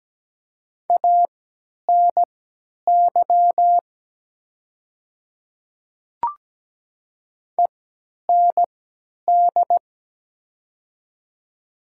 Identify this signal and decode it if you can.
Morse code, heard as a keyed tone, about 17 wpm, spaced as for 10 wpm, 700 Hz. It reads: ANY END
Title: 3x Character Spacing - Top 200 Words - 17wpm